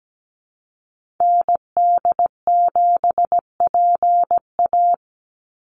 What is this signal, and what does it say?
Morse code, heard as a keyed tone, about 17 wpm, 700 Hz. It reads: ND7PA
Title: NA SOTA Activator Callsigns 17wpm